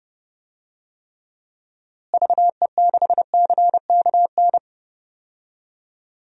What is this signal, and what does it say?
Morse code, heard as a keyed tone, about 30 wpm, 700 Hz. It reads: VE6CKN
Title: DXCC-100 Call Signs - 6 Characters - Repeated 3x - 30wpm